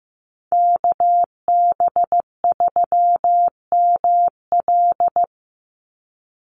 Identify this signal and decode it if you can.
Morse code, heard as a keyed tone, about 15 wpm, 700 Hz. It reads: KB3ML
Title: Call Signs Repeated 3x - 15wpm